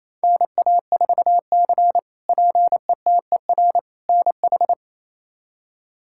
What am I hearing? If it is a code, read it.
Morse code, heard as a keyed tone, about 28 wpm, 700 Hz. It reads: NA4C PETER NH
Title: K1USN SST QSO Practice - 28wpm